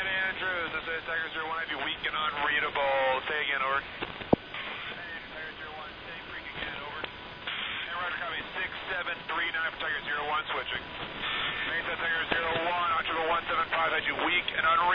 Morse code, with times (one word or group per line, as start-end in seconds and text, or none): none